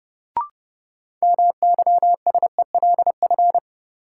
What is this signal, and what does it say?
Morse code, heard as a keyed tone, about 30 wpm, 700 Hz. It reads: MYSELF
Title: Sentences from Top 1300 Words - New Word Drill - 30wpm